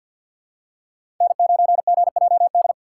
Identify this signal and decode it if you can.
Morse code, heard as a keyed tone, about 53 wpm, 700 Hz. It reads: N9GJD
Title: USA Call Signs - 5 Characters - 8x Character Spacing - 50wpm